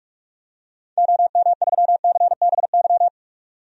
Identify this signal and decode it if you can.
Morse code, heard as a keyed tone, about 46 wpm, 700 Hz. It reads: OM3CBY
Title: DXCC-100 Call Signs - 6 Characters - 45wpm